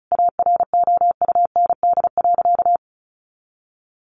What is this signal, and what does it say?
Morse code, heard as a keyed tone, about 34 wpm, 700 Hz. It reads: AROUND.
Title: Easy Phrases 35wpm